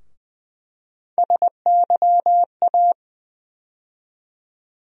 Morse code, heard as a keyed tone, about 20 wpm, 700 Hz. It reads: SYA
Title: USA Call Sign Postfixes - 3 Letters 20wpm